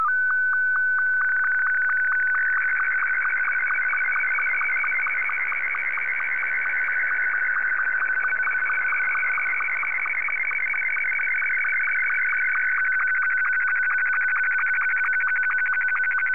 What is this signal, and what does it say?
SSTV (slow-scan television), Martin 2.
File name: sstv / M2-VK3HJV